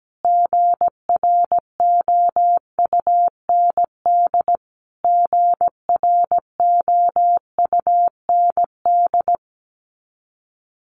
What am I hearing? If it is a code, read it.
Morse code, heard as a keyed tone, about 17 wpm, 700 Hz. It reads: GROUND GROUND
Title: Top 500 Words Repeated 17wpm